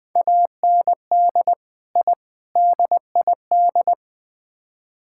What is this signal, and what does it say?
Morse code, heard as a keyed tone, about 21 wpm, 700 Hz. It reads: AND I DID
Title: Build A Phrase - 4-Word Phrases - 20wpm